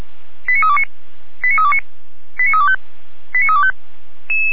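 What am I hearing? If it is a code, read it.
Five-tone selective call (ZVEI), heard as quick runs of five tones, tones of about 70 ms. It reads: 89329 89329 89325 89325
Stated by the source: ZVEI five-tone call received by SDR, from the Feuerwehr Orlamünde audio server.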